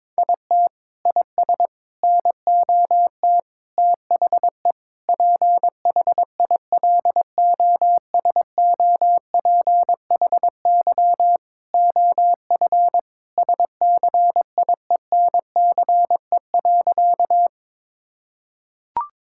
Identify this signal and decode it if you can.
Morse code, heard as a keyed tone, about 22 wpm, 700 Hz. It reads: IT IS NOT THE PHILOSOPHY OF SCIENCE.
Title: Sentences from Top 1000 Words - Review Part 6 - 22wpm